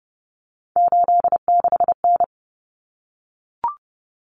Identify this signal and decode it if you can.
Morse code, heard as a keyed tone, about 30 wpm, 700 Hz. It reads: 86N
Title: Common Cut Numbers - 3-Digit 30wpm